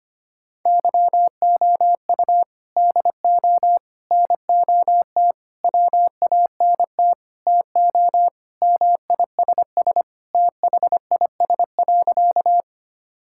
Morse code, heard as a keyed tone, about 25 wpm, 700 Hz. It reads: YOU DO NOT WANT TO MISS THIS.
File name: practice / 25wpm / Sentences with 4-Word Phrases - 25wpm